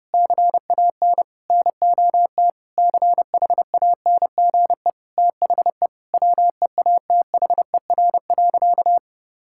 Morse code, heard as a keyed tone, about 32 wpm, 700 Hz. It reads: CAN NOT CHANGE THE WEATHER.
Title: Sentences from Top 1000 Words - Review Part 5 - 30wpm